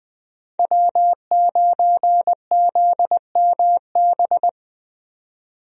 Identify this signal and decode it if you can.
Morse code, heard as a keyed tone, about 20 wpm, 700 Hz. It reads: W9ZMB
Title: USA Call Signs - 5 Characters - 20wpm